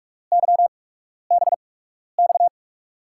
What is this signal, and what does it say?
Morse code, heard as a keyed tone, about 44 wpm, spaced as for 12 wpm, 700 Hz. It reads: YBX
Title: USA Call Signs - 6 Characters - 8x Character Spacing - 45wpm